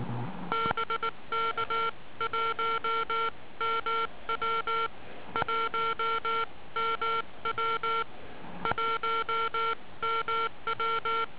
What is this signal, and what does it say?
Morse code, heard as a keyed tone, about 19 wpm, 1360 Hz. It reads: BK 1MW 1MW 1MW